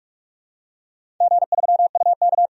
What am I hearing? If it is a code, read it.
Morse code, heard as a keyed tone, about 44 wpm, 700 Hz. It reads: G3UK
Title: DXCC-100 Call Signs - 4 Characters - 45wpm